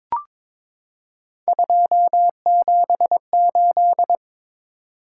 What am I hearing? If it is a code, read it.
Morse code, heard as a keyed tone, about 22 wpm, 700 Hz. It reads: 278
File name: practice / 22wpm / Three Letter-Number Combinations - 22wpm